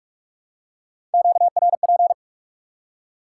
Morse code, heard as a keyed tone, about 43 wpm, 700 Hz. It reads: QRP
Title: Top 180 QSO Elements 45wpm